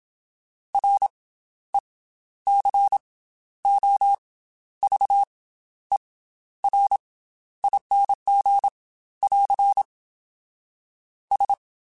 Morse code, heard as a keyed tone, about 27 wpm, 800 Hz. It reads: R E C O V E R ING <AR> S
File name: Special INGs / Special INGs-mp3s / Slow ing @25